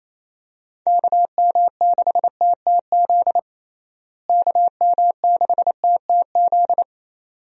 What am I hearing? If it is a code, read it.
Morse code, heard as a keyed tone, about 28 wpm, 700 Hz. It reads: KM6TTZ KM6TTZ
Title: USA Call Signs - 6 Characters - Repeated 2x - 28wpm